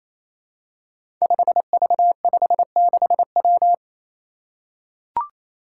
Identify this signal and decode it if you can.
Morse code, heard as a keyed tone, about 27 wpm, 700 Hz. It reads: 5V56W
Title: Five Challenging Letter-Number Combinations 28wpm